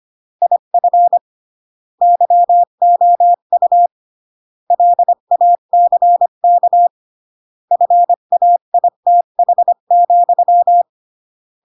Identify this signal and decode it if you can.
Morse code, heard as a keyed tone, about 25 wpm, 700 Hz. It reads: IF YOU LACK FAITH,
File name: CW / jodorowsky / 82 commandments 25-20